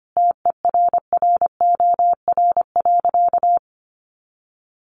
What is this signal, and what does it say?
Morse code, heard as a keyed tone, about 25 wpm, 700 Hz. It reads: TERROR.
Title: A Sound Of Thunder 25wpm